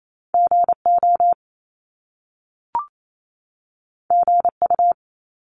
Morse code, heard as a keyed tone, about 28 wpm, 700 Hz. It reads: G O G U
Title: Top Two Letter Combinations in Words 28wpm